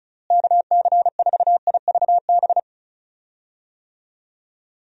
Morse code, heard as a keyed tone, about 36 wpm, 700 Hz. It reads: KC4IVB
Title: USA Call Signs - 6 Characters - Repeated 3x - 35wpm